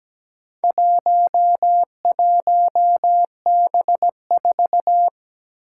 Morse code, heard as a keyed tone, about 17 wpm, 700 Hz. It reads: 11B4
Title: Four Challenging Letter-Number Combinations 17wpm